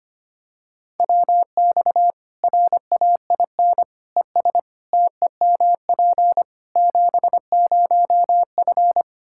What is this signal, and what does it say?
Morse code, heard as a keyed tone, about 25 wpm, 700 Hz. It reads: WX RAIN ES TEMP 70F
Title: Practice QSO - Exchange 2 - Info - 25wpm